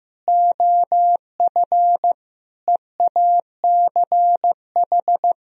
Morse code, heard as a keyed tone, about 15 wpm, 700 Hz. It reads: OF EACH
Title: Sentences from Top 1000 Words - Review Part 3 - 15wpm